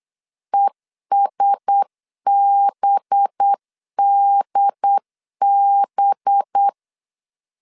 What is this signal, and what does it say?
Morse code, heard as a keyed tone, about 8 wpm, 800 Hz. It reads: ESBDB
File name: CW / DET / 1 6 1500C S B 8wpm 5142